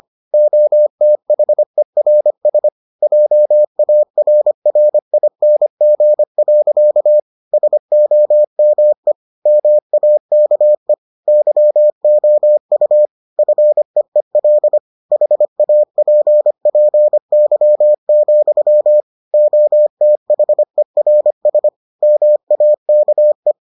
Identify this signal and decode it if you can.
Morse code, heard as a keyed tone, about 25 wpm, 600 Hz. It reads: OTHERS JARRING. SOME MAKE YOU FEEL HAPPY, OTHERS MAKE